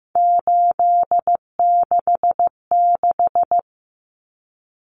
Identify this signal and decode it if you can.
Morse code, heard as a keyed tone, about 15 wpm, 700 Hz. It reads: 866
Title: Common Cut Numbers - 3-Digit 15wpm